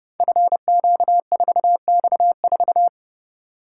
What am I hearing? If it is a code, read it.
Morse code, heard as a keyed tone, about 30 wpm, 700 Hz. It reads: FQ4X4
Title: Code Groups - 5 Letter:Numbers - 30wpm